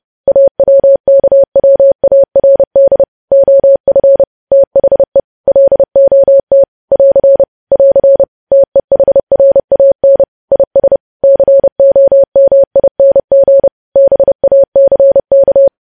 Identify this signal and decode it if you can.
Morse code, heard as a keyed tone, about 30 wpm, 550 Hz. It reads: AWKWARD OF THE LOT <AR> <AR> TEHRAN IS COMING BACK
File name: podcast / News headlines-1x-30x30